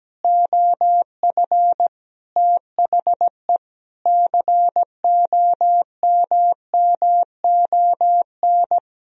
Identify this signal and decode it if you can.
Morse code, heard as a keyed tone, about 17 wpm, 700 Hz. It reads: OF THE COMMON
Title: Sentences from Top 1000 Words - Review Part 3 - 17wpm